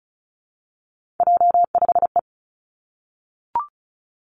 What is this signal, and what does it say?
Morse code, heard as a keyed tone, about 36 wpm, 700 Hz. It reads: J5E
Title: Three Challenging Letter-Number Combinations 35wpm